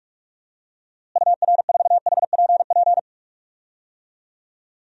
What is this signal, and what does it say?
Morse code, heard as a keyed tone, about 46 wpm, 700 Hz. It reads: UR4HPP
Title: DXCC-100 Call Signs - 6 Characters - Repeated 3x - 45wpm